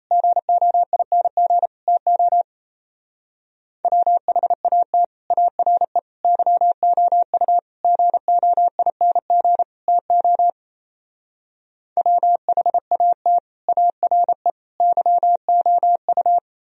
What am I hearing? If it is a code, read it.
Morse code, heard as a keyed tone, about 35 wpm, 700 Hz. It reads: GOING TO WHAT ARE YOU GOING TO WHAT ARE YOU